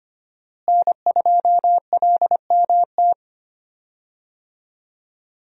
Morse code, heard as a keyed tone, about 25 wpm, 700 Hz. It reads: N2LMT
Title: USA Call Signs - 5 Characters - Repeated 3x - 25wpm